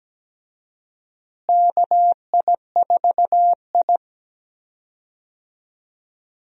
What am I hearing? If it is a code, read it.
Morse code, heard as a keyed tone, about 17 wpm, 700 Hz. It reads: KI4I
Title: USA Call Signs - 4 Characters - Repeated 2x - 17wpm